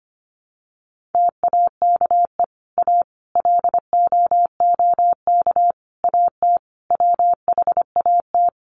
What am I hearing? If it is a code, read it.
Morse code, heard as a keyed tone, about 25 wpm, 700 Hz. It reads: TAKE A LOOK AT WHAT
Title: Sets of 5 Words Encore2 25wpm